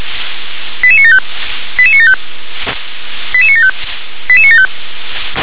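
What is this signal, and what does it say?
Five-tone selective call (ZVEI), heard as quick runs of five tones, tones of about 70 ms. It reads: 88975 88975 88975 88975
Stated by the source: ZVEI five-tone selective call 88960 received by SDR.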